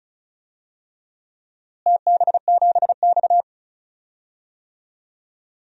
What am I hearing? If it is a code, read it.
Morse code, heard as a keyed tone, about 36 wpm, 700 Hz. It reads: TB7X
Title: DXCC-100 Call Signs - 4 Characters - Repeated 3x - 35wpm